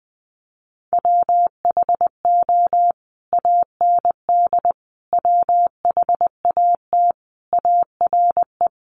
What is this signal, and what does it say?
Morse code, heard as a keyed tone, about 20 wpm, 700 Hz. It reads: WHO AND WHAT ARE
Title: Sentences from Top 100 Words 20wpm